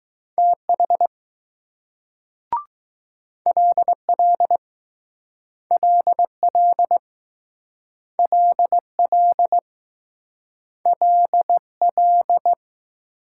Morse code, heard as a keyed tone, about 20 wpm, 700 Hz. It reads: TH LL LL LL LL